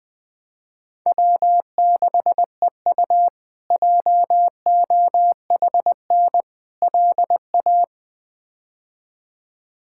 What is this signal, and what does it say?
Morse code, heard as a keyed tone, about 20 wpm, 700 Hz. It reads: W6EU JOHN LA
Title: K1USN SST QSO Practice - 20wpm